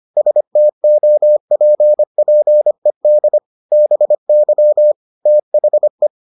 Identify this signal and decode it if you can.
Morse code, heard as a keyed tone, about 25 wpm, 600 Hz. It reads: STOPPED BY THE